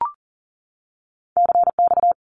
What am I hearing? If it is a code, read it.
Morse code, heard as a keyed tone, about 40 wpm, 700 Hz. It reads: CX
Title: Abbreviations 40wpm